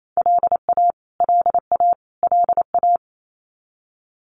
Morse code, heard as a keyed tone, about 28 wpm, 700 Hz. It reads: LA LA LA
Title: US States Repeated 3x 28wpm